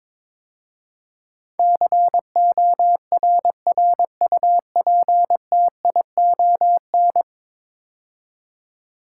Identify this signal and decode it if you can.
Morse code, heard as a keyed tone, about 22 wpm, 700 Hz. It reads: CORRUPTION
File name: practice / 22wpm / 10 Letter Words - First 3 Letters Given - 22wpm